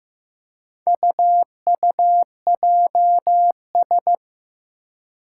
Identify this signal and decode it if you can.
Morse code, heard as a keyed tone, about 15 wpm, 700 Hz. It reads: UUJS
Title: Four Challenging Letter-Number Combinations 15wpm